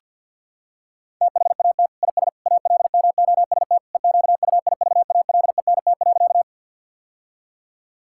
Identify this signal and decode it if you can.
Morse code, heard as a keyed tone, about 50 wpm, 700 Hz. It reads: THAT IS ALMOST EQUIVALENT.
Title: Sentences from Top 1600 Words 50wpm